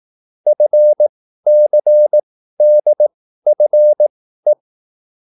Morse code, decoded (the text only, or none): FCDFE